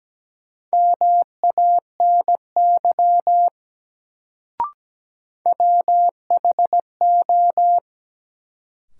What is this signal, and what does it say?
Morse code, heard as a keyed tone, about 17 wpm, 700 Hz. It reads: MANY WHO
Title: Sets of 2 Words - New Word Drill - 17wpm